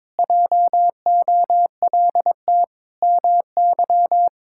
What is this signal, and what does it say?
Morse code, heard as a keyed tone, about 22 wpm, 700 Hz. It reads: JOLT MY
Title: Pangram Challenge v2 - 22wpm